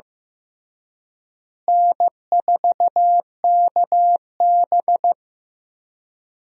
Morse code, heard as a keyed tone, about 16 wpm, 700 Hz. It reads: N4KB
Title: USA Call Signs - 4 Characters - Repeated 2x - 15wpm